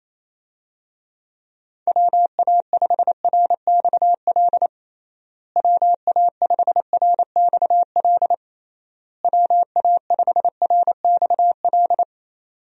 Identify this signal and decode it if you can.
Morse code, heard as a keyed tone, about 27 wpm, 700 Hz. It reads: WA5RXL WA5RXL WA5RXL
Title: USA Call Signs - 6 Characters - Repeated 3x - 28wpm